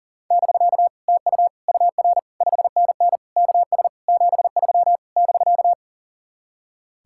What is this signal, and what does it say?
Morse code, heard as a keyed tone, about 54 wpm, spaced as for 47 wpm, 700 Hz. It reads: <BK> TU UR 5NN KS 73 <BK>
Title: Speed-Racing Common QSO Phrases - 40wpm